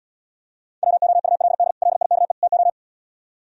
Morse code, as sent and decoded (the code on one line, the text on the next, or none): .-- --- ..- .-.. -.. / .-.. .. -.- . / - ---
WOULD LIKE TO